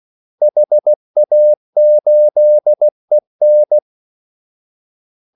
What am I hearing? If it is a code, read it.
Morse code, heard as a keyed tone, about 16 wpm, 600 Hz. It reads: HA8EN
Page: Kaldesignaler 16 wpm